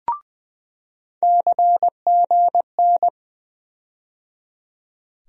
Morse code, heard as a keyed tone, about 21 wpm, 700 Hz. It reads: CGN
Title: USA Call Sign Postfixes - 3 Letters 20wpm